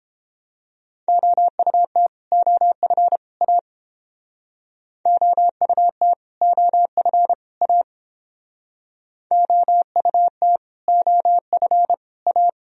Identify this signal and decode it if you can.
Morse code, heard as a keyed tone, about 31 wpm, 700 Hz. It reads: OUT OF A OUT OF A OUT OF A